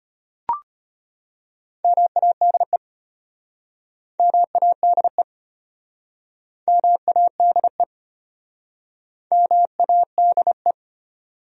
Morse code, heard as a keyed tone, about 35 wpm, 700 Hz. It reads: MADE MADE MADE MADE